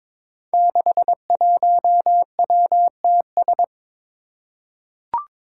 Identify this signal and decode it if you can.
Morse code, heard as a keyed tone, about 22 wpm, 700 Hz. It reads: 61WTS
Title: Five Challenging Letter-Number Combinations 22wpm